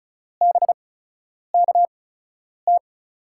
Morse code, heard as a keyed tone, about 36 wpm, spaced as for 10 wpm, 700 Hz. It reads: BKT
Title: USA Call Signs - 8x Character Spacing - 35wpm